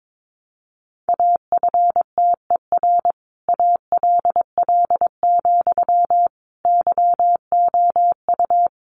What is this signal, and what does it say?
Morse code, heard as a keyed tone, about 22 wpm, 700 Hz. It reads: AFTER ALL, YOU
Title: Sentences from Top 600 Words 22wpm